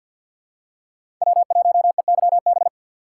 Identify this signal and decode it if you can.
Morse code, heard as a keyed tone, about 52 wpm, 700 Hz. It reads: W1EYB
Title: USA Call Signs - 5 Characters - 3x Character Spacing - 50wpm